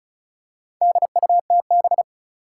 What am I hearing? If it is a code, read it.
Morse code, heard as a keyed tone, about 34 wpm, 700 Hz. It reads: DUTB
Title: Four Challenging Letter-Number Combinations 35wpm